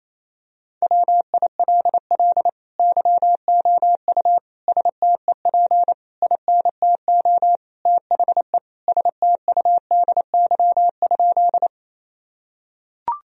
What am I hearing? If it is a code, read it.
Morse code, heard as a keyed tone, about 28 wpm, 700 Hz. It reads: WILL YOU STEP INTO THE STUDY?
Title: Sentences from Top 1000 Words - Review Part 2 - 28wpm